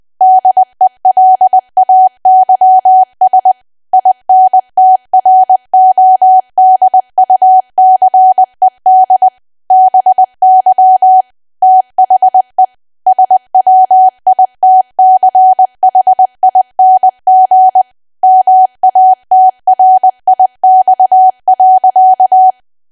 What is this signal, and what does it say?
Morse code, heard as a keyed tone, about 20 wpm, 750 Hz. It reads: DELAYS INTRODUCED BY THE SWITCHING MATRIX.
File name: cwa / W1AW-Level3 / 20WPM